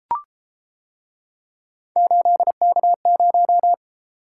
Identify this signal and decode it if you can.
Morse code, heard as a keyed tone, about 31 wpm, 700 Hz. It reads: 8K0